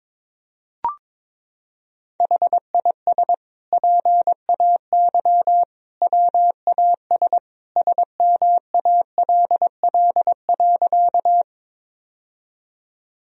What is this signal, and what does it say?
Morse code, heard as a keyed tone, about 22 wpm, 700 Hz. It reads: HIS PAY WAS SMALL.
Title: Sentences from Top 1000 Words - Review Part 1 - 22wpm